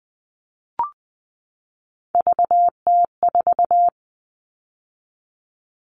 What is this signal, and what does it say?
Morse code, heard as a keyed tone, about 20 wpm, 700 Hz. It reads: VT4
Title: All Cut Numbers - 3-Digit 20wpm